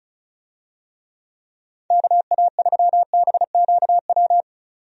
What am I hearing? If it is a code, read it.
Morse code, heard as a keyed tone, about 34 wpm, 700 Hz. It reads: KA3BQW
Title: USA Call Signs - 6 Characters - 35wpm